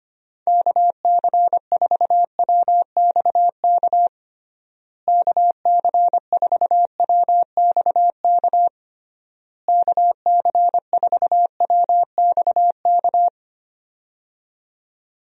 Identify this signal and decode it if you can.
Morse code, heard as a keyed tone, about 25 wpm, 700 Hz. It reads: KC4WXK KC4WXK KC4WXK
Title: USA Call Signs - 6 Characters - Repeated 3x - 25wpm